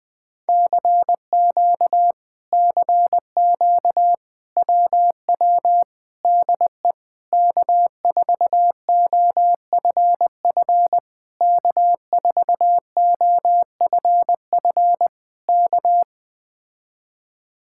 Morse code, heard as a keyed tone, about 20 wpm, 700 Hz. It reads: CQ CQ WW DE K4OFF K4OFF K
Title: Speed-Racing Common QSO Phrases - 17wpm